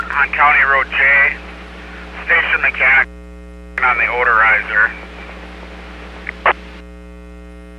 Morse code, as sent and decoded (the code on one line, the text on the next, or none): none